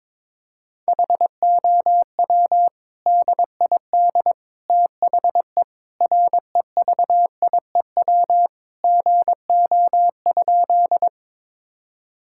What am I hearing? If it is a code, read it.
Morse code, heard as a keyed tone, about 22 wpm, 700 Hz. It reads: HOW DID THE REVIEW GO?
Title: Sentences from Top 1000 Words - Review Part 6 - 22wpm